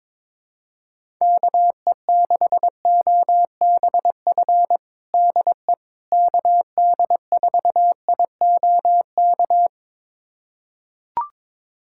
Morse code, heard as a keyed tone, about 22 wpm, 700 Hz. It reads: KE6OBF DE KD4IOK
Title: USA Callsign DE Callsign 22wpm